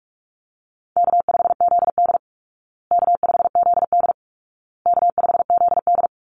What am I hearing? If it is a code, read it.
Morse code, heard as a keyed tone, about 44 wpm, 700 Hz. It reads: K5ZD K5ZD K5ZD
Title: Easy Call Signs Repeated 3x - 45wpm